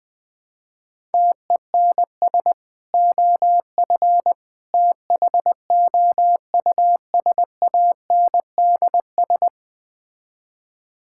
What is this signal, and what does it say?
Morse code, heard as a keyed tone, about 21 wpm, 700 Hz. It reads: TENS OF THOUSANDS
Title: Build A Phrase - 4-Word Phrases - 20wpm